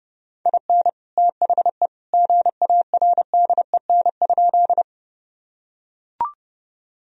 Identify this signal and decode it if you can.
Morse code, heard as a keyed tone, about 30 wpm, 700 Hz. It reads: IN THE GARDEN?
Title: Sentences from Top 1000 Words - Review Part 4 - 30wpm